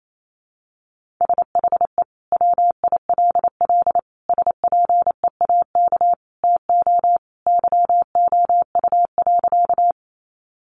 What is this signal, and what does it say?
Morse code, heard as a keyed tone, about 28 wpm, 700 Hz. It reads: SHE WILL SPEAK TO YOU.
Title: Sentences from Top 1100 Words 28wpm